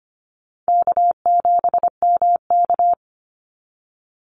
Morse code, heard as a keyed tone, about 25 wpm, 700 Hz. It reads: K7MK
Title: NA SOTA Activator Callsigns 25wpm